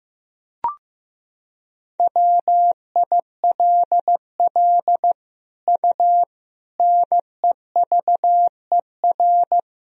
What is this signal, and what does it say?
Morse code, heard as a keyed tone, about 15 wpm, 700 Hz. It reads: WILL U NEVER
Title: Sentences from Top 300 Words and Abbreviations 15wpm